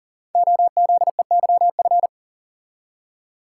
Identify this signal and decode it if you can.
Morse code, heard as a keyed tone, about 41 wpm, 700 Hz. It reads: OZEYF
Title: Code Groups - 5 Letter:Numbers - 40wpm